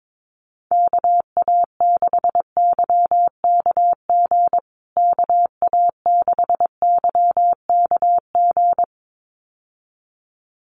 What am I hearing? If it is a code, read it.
Morse code, heard as a keyed tone, about 22 wpm, 700 Hz. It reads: KA6YKG KA6YKG
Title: USA Call Signs Repeated 2x 22wpm